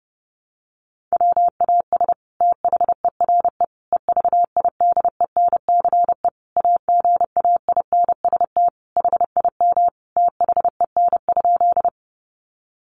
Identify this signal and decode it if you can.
Morse code, heard as a keyed tone, about 30 wpm, 700 Hz. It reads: WAS THERE EVIDENCE AGAINST HIM THEN?
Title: Sentences from Top 1300 Words 30wpm